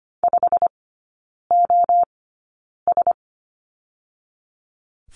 Morse code, heard as a keyed tone, about 25 wpm, 700 Hz. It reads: 5 O S
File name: course / 25:10wpm / 100 Lesson 17e - Letter u - Three Characters 25wpm